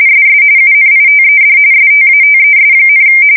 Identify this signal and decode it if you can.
RTTY, about 45.45 baud. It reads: SS UNITED STATES,
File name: fsk170 / battleshipnj